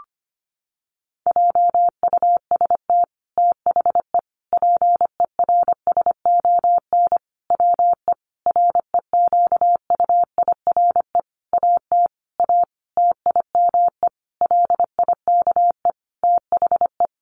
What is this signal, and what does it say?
Morse code, heard as a keyed tone, about 25 wpm, 700 Hz. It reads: JUST THE PERSON WE REQUIRE AT A TIME LIKE THE
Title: Sentences from Top 900 Words 25wpm